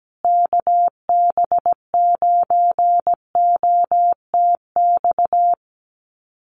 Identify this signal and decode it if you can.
Morse code, heard as a keyed tone, about 17 wpm, 700 Hz. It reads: KB9OTX
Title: USA Call Signs Repeated 2x 17wpm